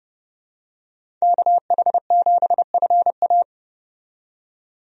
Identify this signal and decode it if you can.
Morse code, heard as a keyed tone, about 31 wpm, 700 Hz. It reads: KH7FA
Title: USA Call Signs - 5 Characters - 30wpm